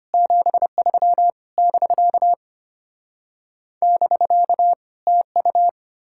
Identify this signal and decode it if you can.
Morse code, heard as a keyed tone, about 33 wpm, 700 Hz. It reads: 73 <BK> <BK> TU